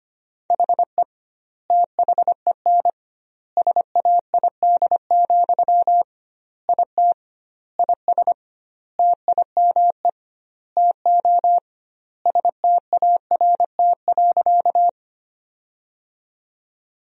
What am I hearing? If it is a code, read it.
Morse code, heard as a keyed tone, about 25 wpm, 700 Hz. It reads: HE THEN SAID, IT IS TIME TO START.
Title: 2x Word-Spacing - Sentences from Top 300 Words - 25wpm